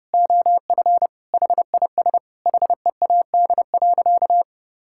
Morse code, heard as a keyed tone, about 30 wpm, 700 Hz. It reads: OF HIS HEAD.
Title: Sentences from Top 1600 Words 30wpm